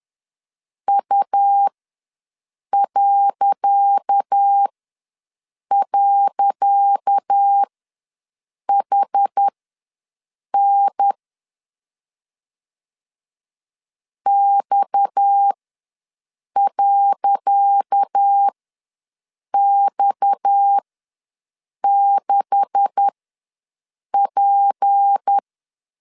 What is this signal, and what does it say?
Morse code, heard as a keyed tone, about 10 wpm, spaced as for 6 wpm, 800 Hz. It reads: U..HN X.X6P
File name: LEZIONI / 10 SIGNS / 10C DOT ws 10wpm 4213